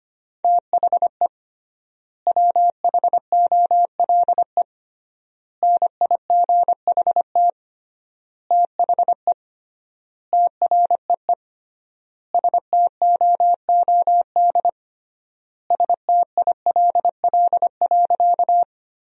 Morse code, heard as a keyed tone, about 25 wpm, 700 Hz. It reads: THE WHOLE NIGHT THE TREE STOOD STILL.
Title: 3x Word-Spacing - Sentences from Top 500 Words - 25wpm